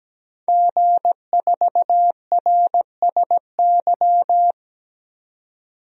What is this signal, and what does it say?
Morse code, heard as a keyed tone, about 17 wpm, 700 Hz. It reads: G4RSY
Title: DXCC-100 Call Signs - 5 Characters - 17wpm